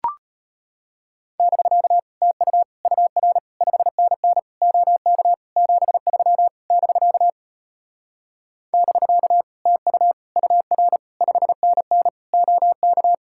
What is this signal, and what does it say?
Morse code, heard as a keyed tone, about 33 wpm, 700 Hz. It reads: <BK> TU UR 5NN OK 73 <BK> <BK> TU UR 5NN OK